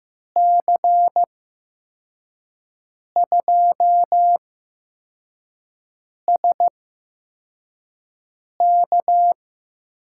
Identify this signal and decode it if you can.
Morse code, heard as a keyed tone, about 15 wpm, spaced as for 4 wpm, 700 Hz. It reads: C2SK
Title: USA Call Signs - 6 Characters - 8x Character Spacing - 15wpm